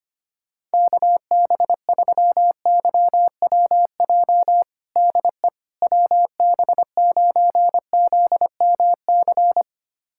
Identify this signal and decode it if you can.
Morse code, heard as a keyed tone, about 25 wpm, 700 Hz. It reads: KB3YWJ DE WB9ZMC
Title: USA Callsign DE Callsign 25wpm